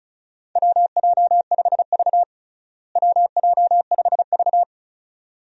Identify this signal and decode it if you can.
Morse code, heard as a keyed tone, about 36 wpm, 700 Hz. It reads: WJ5V WJ5V
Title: USA Call Signs - 4 Characters - Repeated 2x - 35wpm